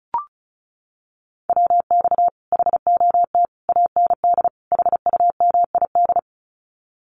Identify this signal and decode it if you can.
Morse code, heard as a keyed tone, about 36 wpm, 700 Hz. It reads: WX HOT AND HUMID
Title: Common QSO Phrases Encore 35wpm